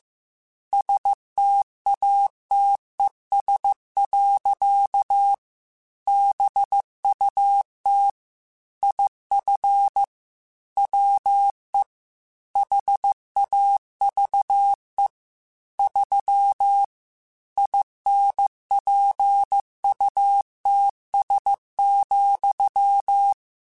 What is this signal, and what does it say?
Morse code, heard as a keyed tone, about 15 wpm, 800 Hz. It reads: STATES. BUT IF WE HAVE 3 INPUTS,
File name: Technical Subjects / Technical Subjects-mp3s / AND Gate @13